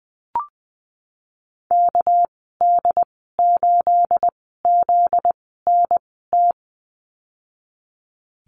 Morse code, heard as a keyed tone, about 20 wpm, spaced as for 15 wpm, 700 Hz. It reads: KD8ZNT
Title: 245 Lesson 39g - Letter z - Call Signs 20wpm